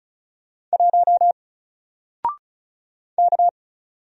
Morse code, heard as a keyed tone, about 34 wpm, 700 Hz. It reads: 1 K
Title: Single Character (Letter:Number:Punctuation:Prosign) 35wpm